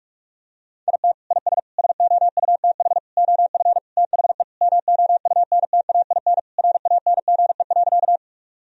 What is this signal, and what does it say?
Morse code, heard as a keyed tone, about 45 wpm, 700 Hz. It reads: IT IS SOUTH OF THE MOUNTAIN RANGE.
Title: Sentences from Top 1000 Words - Review Part 4 - 45wpm